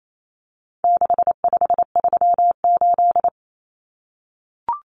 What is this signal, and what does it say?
Morse code, heard as a keyed tone, about 28 wpm, 700 Hz. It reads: 6538
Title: Common Cut Numbers - 4-Digit 28wpm